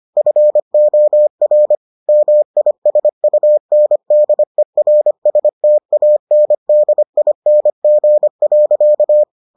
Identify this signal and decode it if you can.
Morse code, heard as a keyed tone, about 25 wpm, 600 Hz. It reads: FOR MISUNDERSTANDING.